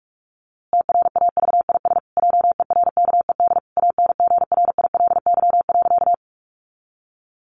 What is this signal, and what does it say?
Morse code, heard as a keyed tone, about 44 wpm, 700 Hz. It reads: TRAVIS JERKED ANGRILY.
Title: A Sound Of Thunder 45wpm